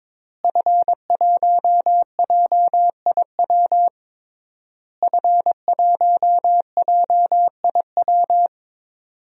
Morse code, heard as a keyed tone, about 22 wpm, 700 Hz. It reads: F1JIW F1JIW
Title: DXCC-100 Call Signs - 5 Characters - Repeated 3x - 22wpm